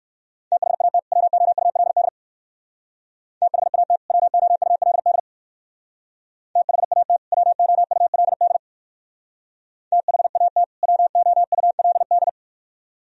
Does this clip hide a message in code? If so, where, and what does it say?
Yes: Morse code from 0.5 to 12.3 s, heard as a keyed tone, about 58 wpm, 700 Hz. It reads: THAT WOULD THAT WOULD THAT WOULD THAT WOULD